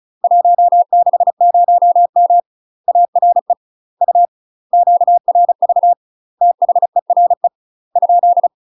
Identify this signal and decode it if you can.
Morse code, heard as a keyed tone, about 34 wpm, 700 Hz. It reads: 160M ARE U QRV THERE ?
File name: UK MASTER QSOs TRAINING / 35WPM / UK MASTER QSOs 0015